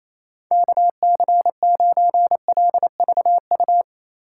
Morse code, heard as a keyed tone, about 28 wpm, 700 Hz. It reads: KC9LVU
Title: USA Call Signs - 6 Characters - 4x Character Spacing - 28wpm